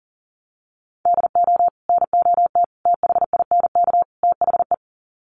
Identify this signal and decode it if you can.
Morse code, heard as a keyed tone, about 40 wpm, 700 Hz. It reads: DO NOT THINK THE
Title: Sets of 4 Words 40wpm